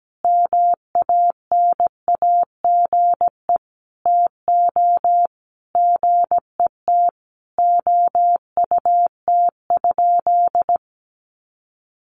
Morse code, heard as a keyed tone, about 17 wpm, 700 Hz. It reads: MANAGE TO GET OUT?
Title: Sentences from Top 1400 Words 17wpm